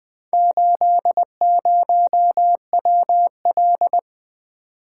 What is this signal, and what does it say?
Morse code, heard as a keyed tone, about 20 wpm, 700 Hz. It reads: 80WL
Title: Four Letter-Number Combinations - 20wpm